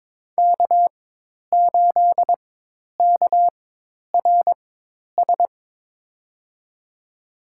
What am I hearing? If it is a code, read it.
Morse code, heard as a keyed tone, about 22 wpm, spaced as for 10 wpm, 700 Hz. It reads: K8KRS
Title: USA Call Signs - 5 Characters - 4x Character Spacing - 22wpm